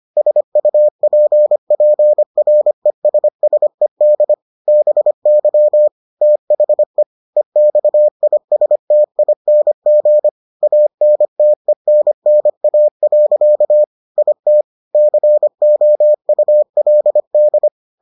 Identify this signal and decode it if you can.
Morse code, heard as a keyed tone, about 25 wpm, 600 Hz. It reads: SUPPRESSED BY THE EXISTING ANTENNA. IT COULD